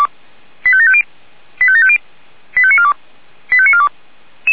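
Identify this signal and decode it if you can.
Five-tone selective call (ZVEI), heard as quick runs of five tones, tones of about 70 ms. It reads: 85750 85750 85942 85942
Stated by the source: ZVEI five-tone call received by SDR.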